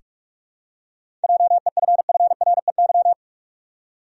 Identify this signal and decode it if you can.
Morse code, heard as a keyed tone, about 45 wpm, 700 Hz. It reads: JEFFREY
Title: USA Common Names - 45wpm